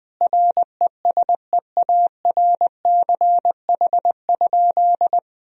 Morse code, heard as a keyed tone, about 21 wpm, 700 Hz. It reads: RESEARCH?
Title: Sentences from Top 1000 Words - Review Part 4 - 20wpm